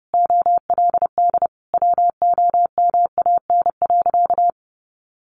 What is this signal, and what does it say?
Morse code, heard as a keyed tone, about 28 wpm, 700 Hz. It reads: OLD WOMAN.
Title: Sentences from Top 1100 Words 30wpm